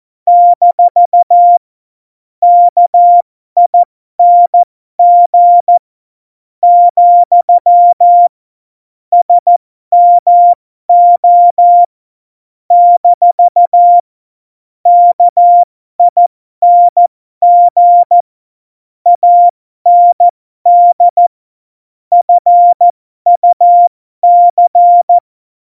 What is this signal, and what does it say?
Morse code, heard as a keyed tone, about 14 wpm, spaced as for 12 wpm, 700 Hz. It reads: - KING , SMO - KING AND FUC
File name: Morsekurs / Witze